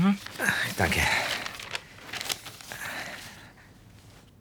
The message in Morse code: none